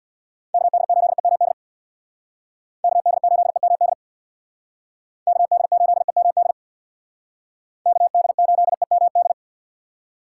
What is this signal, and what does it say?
Morse code, heard as a keyed tone, about 64 wpm, 700 Hz. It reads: KD8EMD KD8EMD KD8EMD KD8EMD